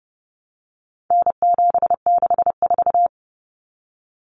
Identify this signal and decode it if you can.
Morse code, heard as a keyed tone, about 30 wpm, 700 Hz. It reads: N764
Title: Common Cut Numbers - 4-Digit 30wpm